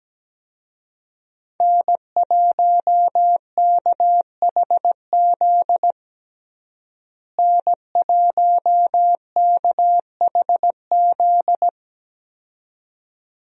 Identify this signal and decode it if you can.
Morse code, heard as a keyed tone, about 17 wpm, 700 Hz. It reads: N1KHZ N1KHZ
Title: USA Call Signs - 5 Characters - Repeated 2x - 17wpm